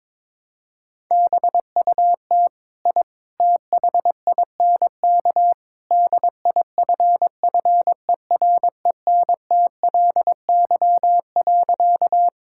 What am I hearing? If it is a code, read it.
Morse code, heard as a keyed tone, about 22 wpm, 700 Hz. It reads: BUT I THINK DIFFERENTLY.
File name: practice / 22wpm / Sentences with 3-Word Phrases - 22wpm